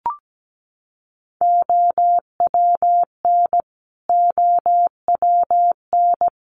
Morse code, heard as a keyed tone, about 17 wpm, 700 Hz. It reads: OWN OWN
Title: Top 500 Words Repeated 17wpm